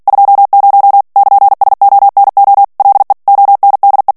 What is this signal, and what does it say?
Morse code, heard as a keyed tone, about 48 wpm, 800 Hz. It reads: 20 <KN>SONO LE ONDE
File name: morse mp3 / ITALIEN texte clair 1 vitesse 45 mots minute